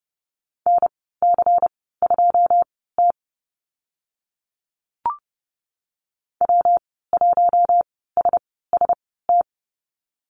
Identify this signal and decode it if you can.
Morse code, heard as a keyed tone, about 30 wpm, spaced as for 17 wpm, 700 Hz. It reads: NC2T W1SST
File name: Lesson 19g - Letter w - Call Signs 30wpm